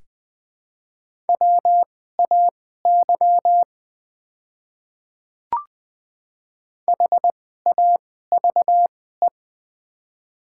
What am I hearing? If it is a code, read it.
Morse code, heard as a keyed tone, about 20 wpm, spaced as for 14 wpm, 700 Hz. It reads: WAY HAVE
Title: 2x Character Spacing - Top 100 Words - 20wpm